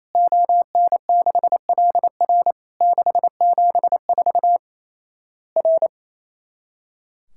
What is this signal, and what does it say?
Morse code, heard as a keyed tone, about 28 wpm, 700 Hz. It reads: ON6LR 674 R
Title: Worked All Europe DX Contest - QTC Exchange - 28wpm